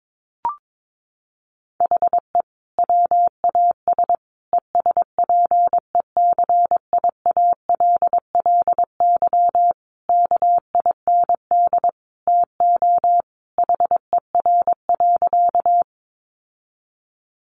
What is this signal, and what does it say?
Morse code, heard as a keyed tone, about 22 wpm, 700 Hz. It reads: HE WAS ESPECIALLY KIND TO HER.
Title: Sentences from Top 700 Words 22wpm